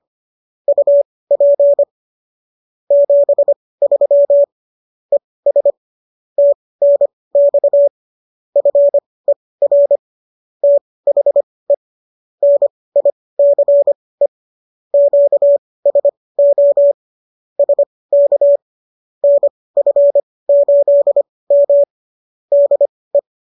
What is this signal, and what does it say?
Morse code, heard as a keyed tone, about 25 wpm, spaced as for 18 wpm, 570 Hz. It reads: UP 73 ES TNX FER THE NICE QSO SK NF8M DE